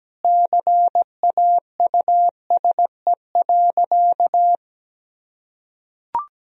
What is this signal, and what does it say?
Morse code, heard as a keyed tone, about 17 wpm, 700 Hz. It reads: CAUSE.
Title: Sentences from Top 1000 Words - Review Part 1 - 17wpm